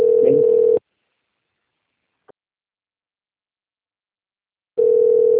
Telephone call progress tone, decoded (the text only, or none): ringback tone